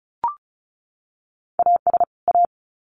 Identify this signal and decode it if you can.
Morse code, heard as a keyed tone, about 34 wpm, 700 Hz. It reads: AS A
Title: Sets of 2 Words 35wpm